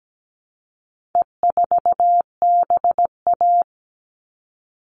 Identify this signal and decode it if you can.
Morse code, heard as a keyed tone, about 17 wpm, 700 Hz. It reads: E4BA
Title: All Cut Numbers - 4-Digit 17wpm